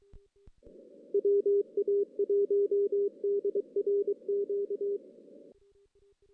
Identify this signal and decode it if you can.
Morse code, heard as a keyed tone, about 25 wpm, 410 Hz. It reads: WA1DRQ
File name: 2016 JAN CW NAQP / 2016 Jan naqp k5ot - 0-6